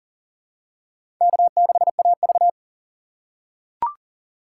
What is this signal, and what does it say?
Morse code, heard as a keyed tone, about 40 wpm, 700 Hz. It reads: K6AV
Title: USA Call Signs - 6x Character Spacing - 40wpm